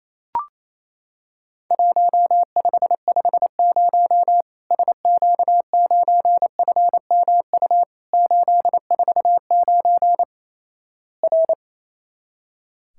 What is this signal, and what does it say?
Morse code, heard as a keyed tone, about 28 wpm, 700 Hz. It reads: 1550 SQ9FMU 849 R
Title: Worked All Europe DX Contest - QTC Exchange - 28wpm